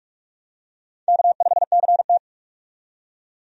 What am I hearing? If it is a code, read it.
Morse code, heard as a keyed tone, about 46 wpm, 700 Hz. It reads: K5CT
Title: USA Call Signs - 4 Characters - 45wpm